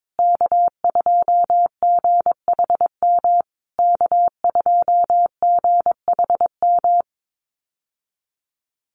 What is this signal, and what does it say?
Morse code, heard as a keyed tone, about 22 wpm, 700 Hz. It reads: K2GHM K2GHM
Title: USA Call Signs Repeated 3x 22wpm